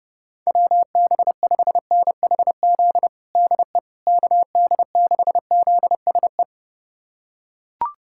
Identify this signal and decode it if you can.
Morse code, heard as a keyed tone, about 32 wpm, 700 Hz. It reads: WB5NHZ DE KD6ZSE
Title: USA Callsign DE Callsign 30wpm